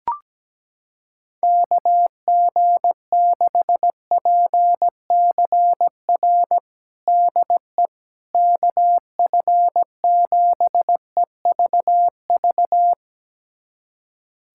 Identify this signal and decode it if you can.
Morse code, heard as a keyed tone, about 17 wpm, 700 Hz. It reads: KG6PCR DE KF7EVV
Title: USA Callsign DE Callsign - 6 Characters - 17wpm